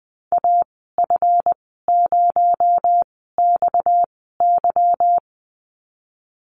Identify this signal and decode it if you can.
Morse code, heard as a keyed tone, about 20 wpm, spaced as for 15 wpm, 700 Hz. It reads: AF0XY
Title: 245 Lesson 39g - Letter z - Call Signs 20wpm